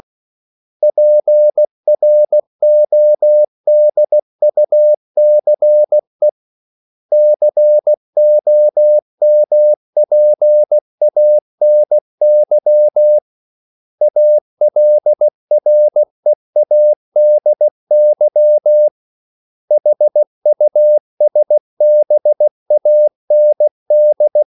Morse code, heard as a keyed tone, about 16 wpm, 600 Hz. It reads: PRODUCE COMPANY ALREADY HUSBAND
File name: CW / Practice / 500MostCommonEnglishWords16